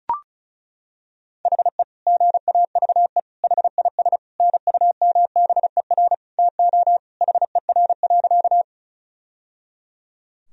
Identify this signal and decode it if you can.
Morse code, heard as a keyed tone, about 36 wpm, 700 Hz. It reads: HE GAVE HIS NUMBER TO HER.
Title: Sentences from Top 1000 Words - Review Part 1 - 35wpm